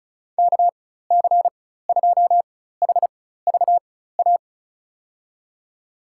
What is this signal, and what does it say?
Morse code, heard as a keyed tone, about 36 wpm, spaced as for 16 wpm, 700 Hz. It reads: KC2HVA
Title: USA Call Signs - 6 Characters - 4x Character Spacing - 35wpm